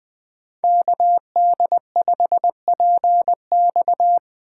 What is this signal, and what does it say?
Morse code, heard as a keyed tone, about 21 wpm, 700 Hz. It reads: KD5PX
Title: Speed Racing - USA 2x2 Call Signs 17wpm